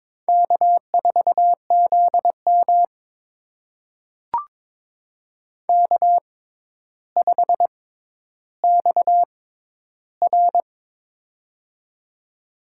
Morse code, heard as a keyed tone, about 22 wpm, spaced as for 11 wpm, 700 Hz. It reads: K4ZM K5XR
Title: USA Call Signs - 4 Characters - 6x Character Spacing - 22wpm